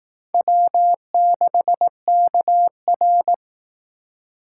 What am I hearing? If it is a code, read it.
Morse code, heard as a keyed tone, about 19 wpm, 700 Hz. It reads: W6KR
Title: Speed Racing - USA 1x2 Call Signs 15wpm